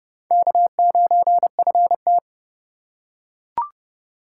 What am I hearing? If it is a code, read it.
Morse code, heard as a keyed tone, about 27 wpm, 700 Hz. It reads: K9FT